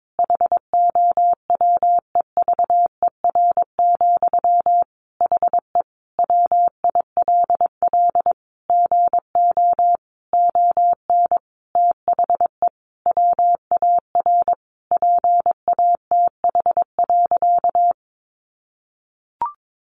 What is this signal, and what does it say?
Morse code, heard as a keyed tone, about 22 wpm, 700 Hz. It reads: HOWEVER, HE WILL GO ON THE WAR PATH.
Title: Sentences from Top 1400 Words 22wpm